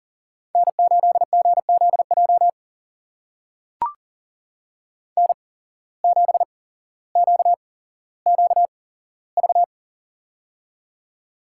Morse code, heard as a keyed tone, about 42 wpm, spaced as for 17 wpm, 700 Hz. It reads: N8GZJ N7QQV
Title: USA Call Signs - 5 Characters - 8x Character Spacing - 40wpm